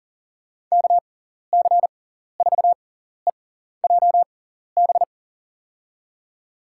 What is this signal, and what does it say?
Morse code, heard as a keyed tone, about 40 wpm, spaced as for 14 wpm, 700 Hz. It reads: KC4EJB
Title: USA Call Signs - 6x Character Spacing - 40wpm